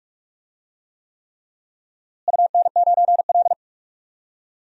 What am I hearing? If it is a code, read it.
Morse code, heard as a keyed tone, about 46 wpm, 690 Hz. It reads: UN9L
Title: DXCC-100 Call Signs - 4 Characters - Repeated 3x - 45wpm